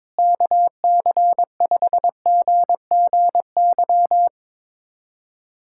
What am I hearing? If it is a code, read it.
Morse code, heard as a keyed tone, about 22 wpm, 700 Hz. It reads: KC5GGY
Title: USA Callsign DE Callsign - 6 Characters - 22wpm